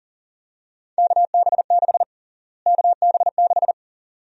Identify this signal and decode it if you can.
Morse code, heard as a keyed tone, about 40 wpm, 700 Hz. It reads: KB6 KB6
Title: USA Call Sign Prefixes - Repeated 2x - 40wpm